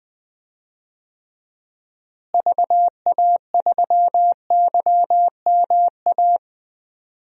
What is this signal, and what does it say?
Morse code, heard as a keyed tone, about 20 wpm, 700 Hz. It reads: VA3YMA
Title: DXCC-100 Call Signs - 6 Characters - Repeated 2x - 20wpm